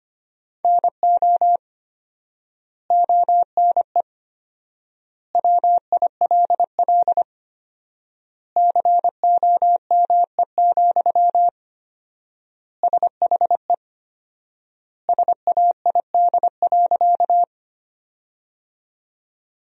Morse code, heard as a keyed tone, about 25 wpm, 700 Hz. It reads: NO ONE WILL COME, SHE SAID.
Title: 4x Word-Spacing - Sentences from Top 100 Words - 25wpm